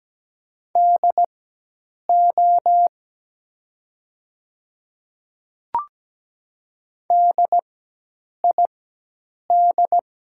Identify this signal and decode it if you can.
Morse code, heard as a keyed tone, about 17 wpm, spaced as for 8 wpm, 700 Hz. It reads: DO DID
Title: 4x Character Spacing - Top 100 Words - 17wpm